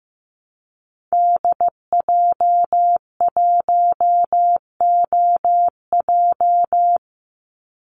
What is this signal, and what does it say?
Morse code, heard as a keyed tone, about 15 wpm, 700 Hz. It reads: DJ1OJ
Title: Call Signs Repeated 2x - 15wpm